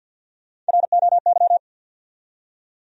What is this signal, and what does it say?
Morse code, heard as a keyed tone, about 51 wpm, 700 Hz. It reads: ROY
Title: USA Common Names - Max 3 Letters - 50wpm